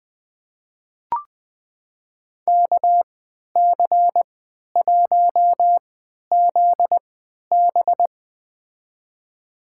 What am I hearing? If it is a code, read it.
Morse code, heard as a keyed tone, about 21 wpm, spaced as for 12 wpm, 700 Hz. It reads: KC1ZB
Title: USA Call Signs - 3x Character Spacing - 20wpm